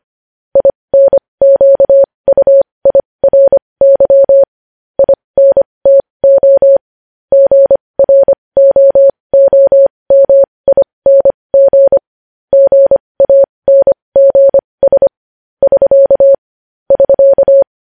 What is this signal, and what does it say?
Morse code, heard as a keyed tone, about 25 wpm, spaced as for 20 wpm, 550 Hz. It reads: INQUIRY INTO GROOMING GANGS <SK> <SK>
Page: news headlines in morse code at 20 wpm